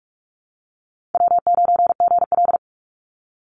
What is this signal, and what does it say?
Morse code, heard as a keyed tone, about 44 wpm, 700 Hz. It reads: W9GL
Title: Easy Call Signs - 45wpm